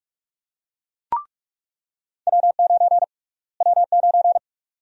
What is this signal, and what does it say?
Morse code, heard as a keyed tone, about 46 wpm, 700 Hz. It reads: W9 W9
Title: USA Call Sign Prefixes - Repeated 2x - 45wpm